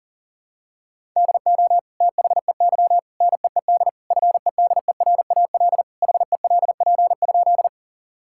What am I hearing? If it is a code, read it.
Morse code, heard as a keyed tone, about 40 wpm, 700 Hz. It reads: DO THEY NEED FEDERAL HELP?
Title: Sentences from Top 1000 Words - Review Part 5 - 40wpm